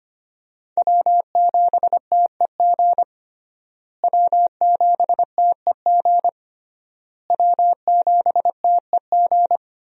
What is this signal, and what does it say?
Morse code, heard as a keyed tone, about 25 wpm, 700 Hz. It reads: W7TEG W7TEG W7TEG
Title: USA Call Signs - 5 Characters - Repeated 3x - 25wpm